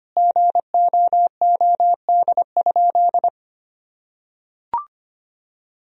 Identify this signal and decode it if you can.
Morse code, heard as a keyed tone, about 25 wpm, 700 Hz. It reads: GOOD?
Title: Sentences from Top 1000 Words - Review Part 4 - 25wpm